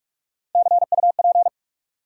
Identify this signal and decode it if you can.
Morse code, heard as a keyed tone, about 46 wpm, 700 Hz. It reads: CUP
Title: Sentences from Top 1600 Words - New Word Drill Alt - 45wpm